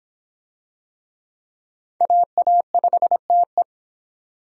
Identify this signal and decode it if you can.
Morse code, heard as a keyed tone, about 25 wpm, 700 Hz. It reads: AA5TE